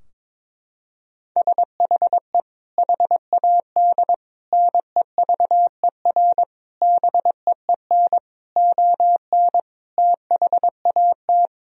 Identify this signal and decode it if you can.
Morse code, heard as a keyed tone, about 22 wpm, 700 Hz. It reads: SHE HAD NEVER BEEN ON THAT
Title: Sentences from Top 1000 Words - Review Part 2 - 22wpm